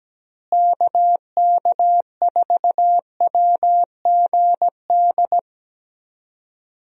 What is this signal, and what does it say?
Morse code, heard as a keyed tone, about 17 wpm, 700 Hz. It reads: KK4WGD
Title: USA Call Signs - 6 Characters - Repeated 3x - 17wpm